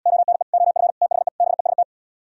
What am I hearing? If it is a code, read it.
Morse code, heard as a keyed tone, about 77 wpm, 700 Hz. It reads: ONE OF THE BEST